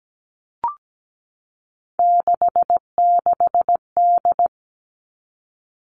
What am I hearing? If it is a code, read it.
Morse code, heard as a keyed tone, about 17 wpm, 700 Hz. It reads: 66D
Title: All Cut Numbers - 3-Digit 17wpm